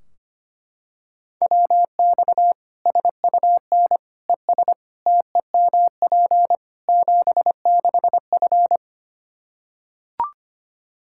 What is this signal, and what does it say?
Morse code, heard as a keyed tone, about 25 wpm, 700 Hz. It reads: WX SUN ES TEMP 76F
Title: K1USN SST QSO Practice 25wpm